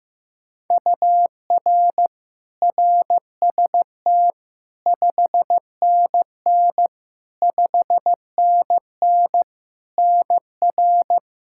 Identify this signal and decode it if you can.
Morse code, heard as a keyed tone, about 15 wpm, 700 Hz. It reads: UR RST 5NN 5NN NR
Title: Practice QSO - Exchange 1 - Beginning - 15wpm